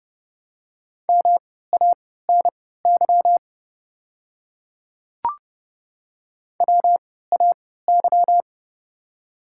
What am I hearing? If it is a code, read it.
Morse code, heard as a keyed tone, about 30 wpm, spaced as for 17 wpm, 700 Hz. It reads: MANY WAY
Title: 3x Character Spacing - Top 100 Words - 30wpm